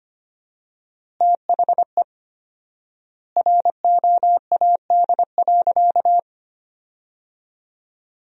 Morse code, heard as a keyed tone, about 25 wpm, 700 Hz. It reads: THE ROAD.
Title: 4x Word-Spacing - Sentences from Top 400 Words - 25wpm